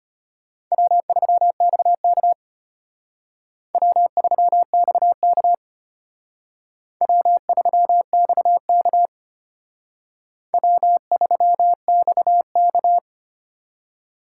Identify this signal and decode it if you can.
Morse code, heard as a keyed tone, about 31 wpm, 700 Hz. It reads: W3XK W3XK W3XK W3XK